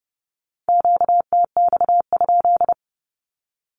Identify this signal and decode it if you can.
Morse code, heard as a keyed tone, about 30 wpm, 700 Hz. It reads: QTX?
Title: Q-Codes with Spoken Meaning - 30wpm